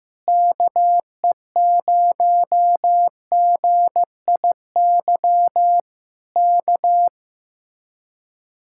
Morse code, heard as a keyed tone, about 15 wpm, 700 Hz. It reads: KE0GIY K
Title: Speed-Racing Common QSO Phrases Encore - 15wpm